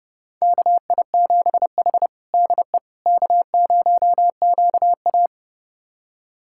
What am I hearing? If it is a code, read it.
Morse code, heard as a keyed tone, about 32 wpm, 700 Hz. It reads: KI7H DE K0QA
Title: USA Callsign DE Callsign - 4 Characters - 30wpm